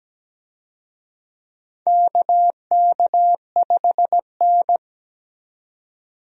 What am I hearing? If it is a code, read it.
Morse code, heard as a keyed tone, about 17 wpm, 700 Hz. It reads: KK5N
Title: USA Call Signs - 4 Characters - 17wpm